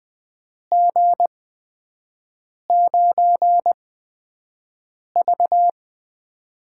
Morse code, heard as a keyed tone, about 20 wpm, spaced as for 5 wpm, 700 Hz. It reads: G9V
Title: USA Call Signs - 4 Characters - 8x Character Spacing - 20wpm